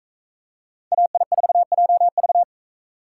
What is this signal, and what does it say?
Morse code, heard as a keyed tone, about 41 wpm, 700 Hz. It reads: AI4JV